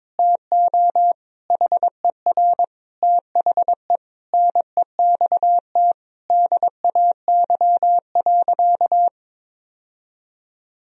Morse code, heard as a keyed tone, about 22 wpm, 700 Hz. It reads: TO HER THE NEXT DAY.
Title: Sentences from Top 1000 Words - Review Part 1 - 22wpm